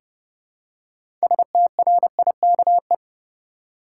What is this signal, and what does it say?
Morse code, heard as a keyed tone, about 30 wpm, 700 Hz. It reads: STRIKE
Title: Six Letter Words - 30wpm